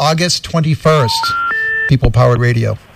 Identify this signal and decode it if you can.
Special information tone, with speech.